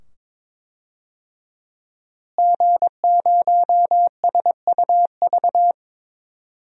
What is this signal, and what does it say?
Morse code, heard as a keyed tone, about 21 wpm, 700 Hz. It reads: G0SUV
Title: DXCC-100 Call Signs - 5 Characters - Repeated 2x - 22wpm